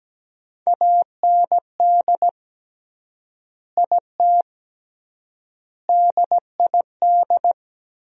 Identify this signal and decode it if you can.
Morse code, heard as a keyed tone, about 17 wpm, 700 Hz. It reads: AND IT DID
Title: 3x Word-Spacing - Sentences from Top 200 Words - 17wpm